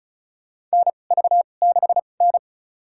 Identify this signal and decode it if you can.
Morse code, heard as a keyed tone, about 34 wpm, spaced as for 25 wpm, 690 Hz. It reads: NV6N
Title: USA Call Signs - 4 Characters - 2x Character Spacing - 35wpm